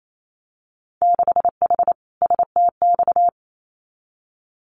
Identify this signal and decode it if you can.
Morse code, heard as a keyed tone, about 28 wpm, 700 Hz. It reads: 6H STX
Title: Winter Field Day Exchanges Encore 28wpm